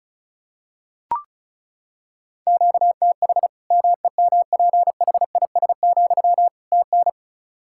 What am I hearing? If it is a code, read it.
Morse code, heard as a keyed tone, about 36 wpm, 700 Hz. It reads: QTH MEMPHIS, TN
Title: Practice QSO - Exchange 1 - Beginning - 35wpm